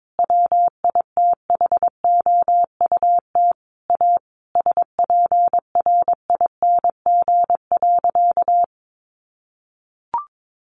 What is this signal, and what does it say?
Morse code, heard as a keyed tone, about 22 wpm, 700 Hz. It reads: WITHOUT A SPRING.
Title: Sentences from Top 1200 Words 22wpm